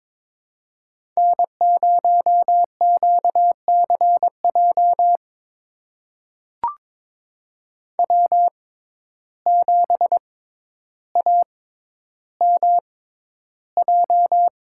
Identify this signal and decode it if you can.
Morse code, heard as a keyed tone, about 22 wpm, spaced as for 11 wpm, 700 Hz. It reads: N0QCJ W7AMJ
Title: USA Call Signs - 5 Characters - 6x Character Spacing - 22wpm